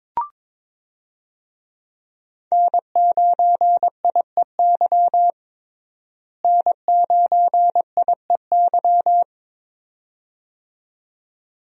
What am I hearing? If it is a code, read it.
Morse code, heard as a keyed tone, about 22 wpm, 700 Hz. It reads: N9IEY N9IEY
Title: USA Call Signs - 5 Characters - Repeated 2x - 22wpm